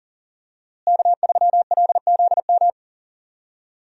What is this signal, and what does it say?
Morse code, heard as a keyed tone, about 40 wpm, 700 Hz. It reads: K3LZM
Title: USA Call Signs - 5 Characters - Repeated 3x - 40wpm